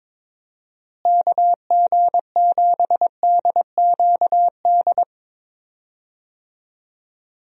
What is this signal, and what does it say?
Morse code, heard as a keyed tone, about 22 wpm, 700 Hz. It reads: KG7DQD
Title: USA Call Signs - 6 Characters - Repeated 2x - 22wpm